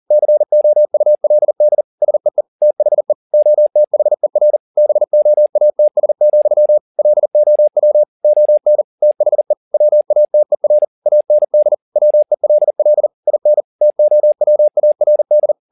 Morse code, heard as a keyed tone, about 40 wpm, 600 Hz. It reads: COULD SEE THE OTHER BOATS, LOW ON THE WATER AND WELL IN TOWARD